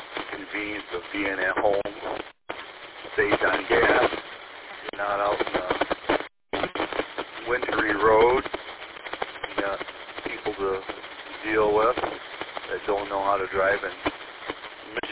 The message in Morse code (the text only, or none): none